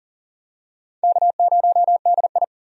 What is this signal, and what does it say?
Morse code, heard as a keyed tone, about 40 wpm, 700 Hz. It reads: K0DI
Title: USA Call Signs - 4 Characters - 40wpm